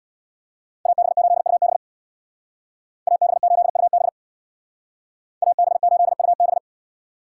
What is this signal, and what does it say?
Morse code, heard as a keyed tone, about 64 wpm, 700 Hz. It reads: AB8UB AB8UB AB8UB